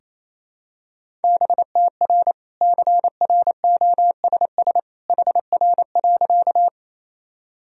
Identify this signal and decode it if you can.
Morse code, heard as a keyed tone, about 28 wpm, 700 Hz. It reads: BTR CROSS HR.
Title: Sentences from Top 400 Words and Abbreviations 28wpm